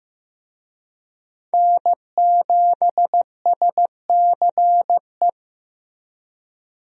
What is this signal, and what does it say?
Morse code, heard as a keyed tone, about 15 wpm, 700 Hz. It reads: N7SCE
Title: USA Call Signs - 5 Characters - Repeated 3x - 15wpm